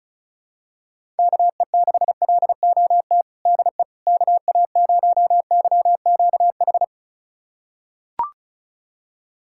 Morse code, heard as a keyed tone, about 34 wpm, 700 Hz. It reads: KE6LOT DE KA0YQH
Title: USA Callsign DE Callsign 35wpm